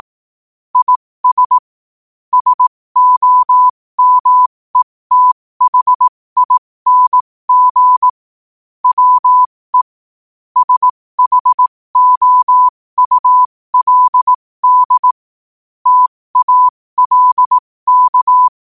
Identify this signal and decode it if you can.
Morse code, heard as a keyed tone, about 18 wpm, spaced as for 15 wpm, 1000 Hz. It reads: IS SOMETHING WE SHOULD TALK
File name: qotdfiles / 15wpm